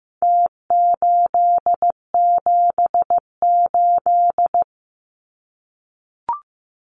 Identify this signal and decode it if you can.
Morse code, heard as a keyed tone, about 15 wpm, 700 Hz. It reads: T878
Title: Numbers - Contest Style - 4-Digit 15wpm